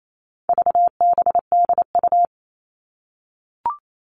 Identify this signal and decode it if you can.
Morse code, heard as a keyed tone, about 28 wpm, 700 Hz. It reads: VBDU
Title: All Cut Numbers - 4-Digit 28wpm